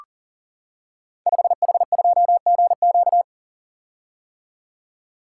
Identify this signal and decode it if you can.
Morse code, heard as a keyed tone, about 38 wpm, 700 Hz. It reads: 5H2GQ
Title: Code Groups - 5 Letter:Numbers - 40wpm